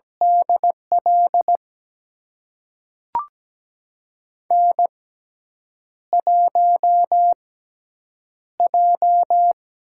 Morse code, heard as a keyed tone, about 17 wpm, spaced as for 7 wpm, 700 Hz. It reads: DL N1J
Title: USA Call Signs - 6x Character Spacing - 17wpm